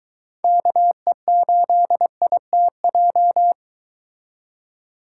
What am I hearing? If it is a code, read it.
Morse code, heard as a keyed tone, about 23 wpm, 700 Hz. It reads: KE8ITJ